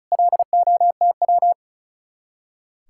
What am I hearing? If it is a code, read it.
Morse code, heard as a keyed tone, about 44 wpm, 700 Hz. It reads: LOTW